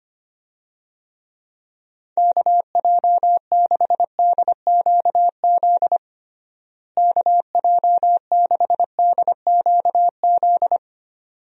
Morse code, heard as a keyed tone, about 25 wpm, 700 Hz. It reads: KJ6DQZ KJ6DQZ
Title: USA Call Signs - 6 Characters - Repeated 2x - 25wpm